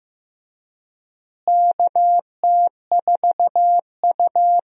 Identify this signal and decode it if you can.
Morse code, heard as a keyed tone, about 15 wpm, 700 Hz. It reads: KT4U
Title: USA Call Signs - 4 Characters - Repeated 3x - 15wpm